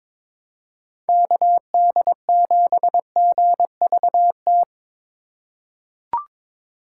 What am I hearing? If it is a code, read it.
Morse code, heard as a keyed tone, about 21 wpm, 700 Hz. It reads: KD7GVT
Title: USA Call Signs - 6 Characters - 3x Character Spacing - 22wpm